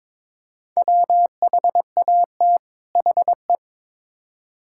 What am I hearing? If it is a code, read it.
Morse code, heard as a keyed tone, about 22 wpm, 700 Hz. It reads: WHAT HE
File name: Build A Phrase - 2-Word Phrases - 22wpm